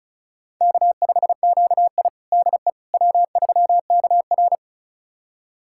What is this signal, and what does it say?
Morse code, heard as a keyed tone, about 36 wpm, 700 Hz. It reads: K5QI DE W3KR
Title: USA Callsign DE Callsign - 4 Characters - 35wpm